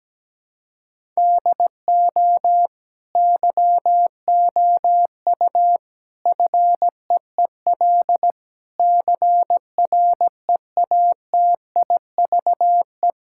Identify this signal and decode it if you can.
Morse code, heard as a keyed tone, about 17 wpm, 700 Hz. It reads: DO YOU FEEL CREATIVE
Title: Sentences from Top 1000 Words - Review Part 5 - 17wpm